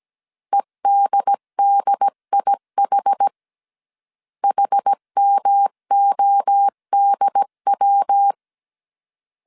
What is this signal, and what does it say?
Morse code, heard as a keyed tone, about 17 wpm, 800 Hz. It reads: EDDIH HMODW
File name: CW / DET / 1 3 1500C S A 16wpm 2141